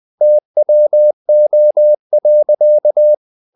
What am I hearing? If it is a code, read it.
Morse code, heard as a keyed tone, about 20 wpm, 600 Hz. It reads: TWO.
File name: the old man and the sea / the old man and the sea part 1 20wpm